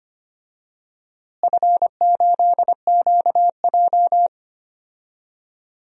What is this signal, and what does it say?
Morse code, heard as a keyed tone, about 25 wpm, 700 Hz. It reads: F8QJ
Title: DXCC-100 Call Signs - 4 Characters - Repeated 3x - 25wpm